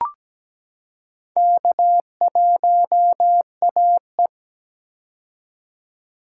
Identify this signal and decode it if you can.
Morse code, heard as a keyed tone, about 17 wpm, 700 Hz. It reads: K1AE
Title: K1USN SST QSO Practice - 17wpm